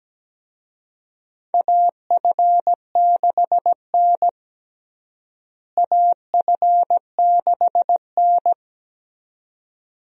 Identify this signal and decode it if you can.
Morse code, heard as a keyed tone, about 17 wpm, 700 Hz. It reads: AF6N AF6N
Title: USA Call Signs - 4 Characters - Repeated 2x - 17wpm